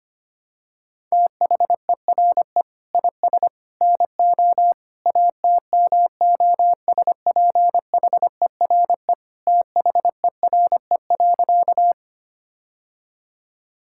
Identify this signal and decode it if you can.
Morse code, heard as a keyed tone, about 25 wpm, 700 Hz. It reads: THERE IS NO ATMOSPHERE THERE.
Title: Sentences from Top 1600 Words 25wpm